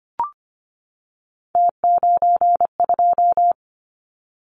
Morse code, heard as a keyed tone, about 25 wpm, 700 Hz. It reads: T92
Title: Callsign Prefixes for Top 100 DXCC - 25wpm